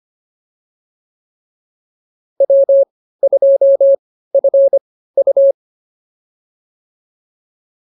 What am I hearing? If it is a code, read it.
Morse code, heard as a keyed tone, about 25 wpm, spaced as for 15 wpm, 550 Hz. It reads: W2FU
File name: POTA105 15